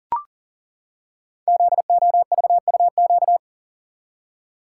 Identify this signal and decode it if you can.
Morse code, heard as a keyed tone, about 40 wpm, 700 Hz. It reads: ZOVUQ
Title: Code Groups - 5 Letters - 40wpm